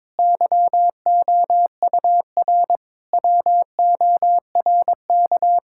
Morse code, heard as a keyed tone, about 22 wpm, 700 Hz. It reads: YOUR WORK
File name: Difficult Sentences with Hints - 22wpm